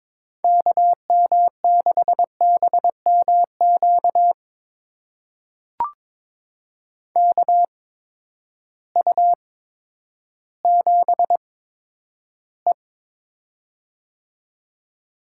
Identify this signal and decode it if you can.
Morse code, heard as a keyed tone, about 22 wpm, spaced as for 11 wpm, 700 Hz. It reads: KM6BMQ KU7E
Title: USA Call Signs - 8x Character Spacing - 22wpm